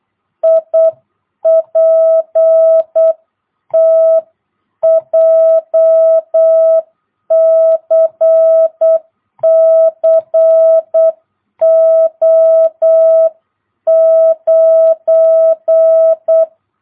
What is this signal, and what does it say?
Morse code, heard as a keyed tone, about 8 wpm, 660 Hz. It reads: IPT JCC O9
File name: jh8cbh / 2022cw